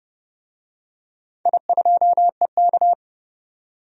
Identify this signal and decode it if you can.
Morse code, heard as a keyed tone, about 30 wpm, 700 Hz. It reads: I2EK
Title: DXCC-100 Call Signs - 4 Characters - 30wpm